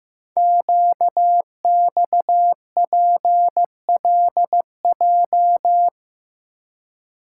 Morse code, heard as a keyed tone, about 15 wpm, 700 Hz. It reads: QXPLJ
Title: Code Groups - 5 Letters - 15wpm